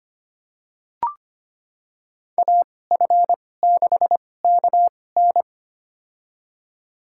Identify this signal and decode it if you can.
Morse code, heard as a keyed tone, about 25 wpm, spaced as for 18 wpm, 700 Hz. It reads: AF6KN
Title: USA Call Signs - 5 Characters - 2x Character Spacing - 25wpm